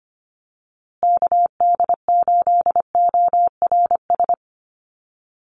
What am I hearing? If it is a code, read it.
Morse code, heard as a keyed tone, about 25 wpm, 700 Hz. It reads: KD8ORS
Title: USA Call Signs Repeated 2x 25wpm